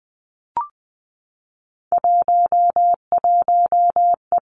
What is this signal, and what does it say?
Morse code, heard as a keyed tone, about 20 wpm, 700 Hz. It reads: 11E
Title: Three Challenging Letter-Number Combinations 20wpm